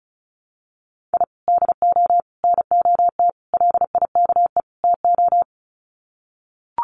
Morse code, heard as a keyed tone, about 34 wpm, 700 Hz. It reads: I DO NOT LIKE TO
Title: Sets of 5 Words 35wpm